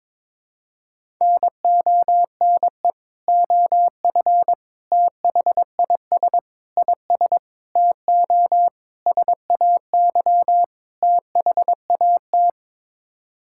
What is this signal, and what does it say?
Morse code, heard as a keyed tone, about 22 wpm, 700 Hz. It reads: NONE OF THIS IS TO SAY THAT
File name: practice / 22wpm / Sets of 7 Words - 22wpm